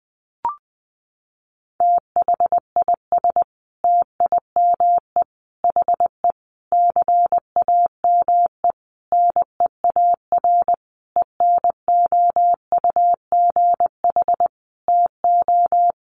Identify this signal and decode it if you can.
Morse code, heard as a keyed tone, about 20 wpm, 700 Hz. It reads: THIS TIME HE CAME NEAR ENOUGH TO
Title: Sentences from Top 1200 Words 20wpm